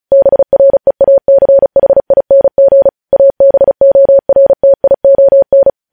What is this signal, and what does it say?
Morse code, heard as a keyed tone, about 36 wpm, 550 Hz. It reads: BREACHING ABORTION
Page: news headlines in morse code at 35 wpm